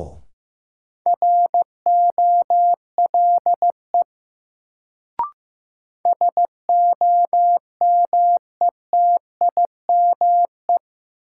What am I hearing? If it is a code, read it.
Morse code, heard as a keyed tone, about 15 wpm, 700 Hz. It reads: ROLE SOMETIME
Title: Sentences from Top 1000 Words - Review Part 6 - New Word Drill - 15wpm